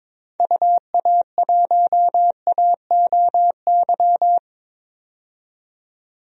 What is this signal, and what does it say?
Morse code, heard as a keyed tone, about 21 wpm, 700 Hz. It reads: UA1AOY
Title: DXCC-100 Call Signs - 6 Characters - 22wpm